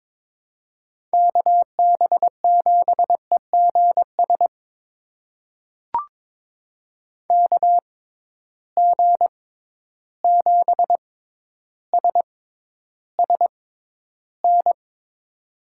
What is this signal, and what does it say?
Morse code, heard as a keyed tone, about 22 wpm, spaced as for 11 wpm, 700 Hz. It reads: KB7EGS KG7SSN
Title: USA Call Signs - 6x Character Spacing - 22wpm